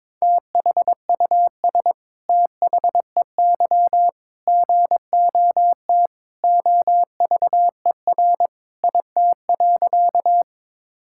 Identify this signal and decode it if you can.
Morse code, heard as a keyed tone, about 22 wpm, 700 Hz. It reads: THUS THEY GOT OVER IT.
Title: Sentences from Top 1000 Words - Review Part 5 - 22wpm